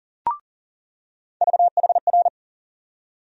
Morse code, heard as a keyed tone, about 40 wpm, 700 Hz. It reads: VHR
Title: USA Call Sign Postfixes - 3 Letters 40wpm